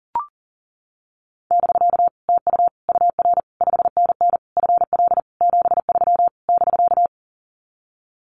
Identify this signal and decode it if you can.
Morse code, heard as a keyed tone, about 40 wpm, 700 Hz. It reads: <BK> TU UR 5NN FL 73 <BK>
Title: Common QSO Phrases Encore 40wpm